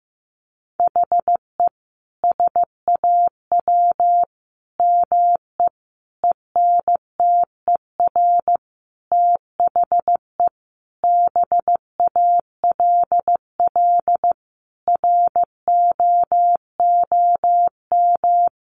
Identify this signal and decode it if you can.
Morse code, heard as a keyed tone, about 14 wpm, 700 Hz. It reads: HE SAW ME ENTER THE BALL ROOM
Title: Sentences from Top 1300 Words 15wpm